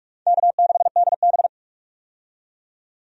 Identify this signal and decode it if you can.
Morse code, heard as a keyed tone, about 45 wpm, 700 Hz. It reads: K6DB
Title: USA Call Signs - 4 Characters - 45wpm